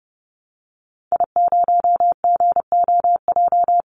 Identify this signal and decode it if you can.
Morse code, heard as a keyed tone, about 30 wpm, 700 Hz. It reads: I0GOJ
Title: Call Signs - 30wpm